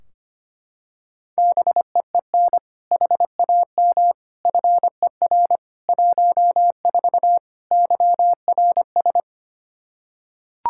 Morse code, heard as a keyed tone, about 25 wpm, 700 Hz. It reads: BEEN HAM FER 14 YRS
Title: Practice QSO - Exchange 3 - More Info - 25wpm